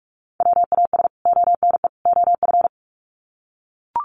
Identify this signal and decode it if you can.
Morse code, heard as a keyed tone, about 44 wpm, 700 Hz. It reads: WAS ONE OF
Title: Sets of 3 Words 45wpm